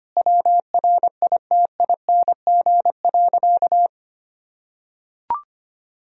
Morse code, heard as a keyed tone, about 25 wpm, 700 Hz. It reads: WRITING.
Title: Sentences from Top 1000 Words - Review Part 1 - 25wpm